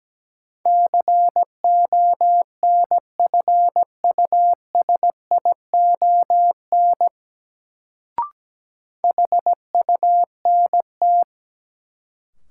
Morse code, heard as a keyed tone, about 17 wpm, 700 Hz. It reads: CONFUSION HUNT
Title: Sentences from Top 1600 Words - New Word Drill Alt - 17wpm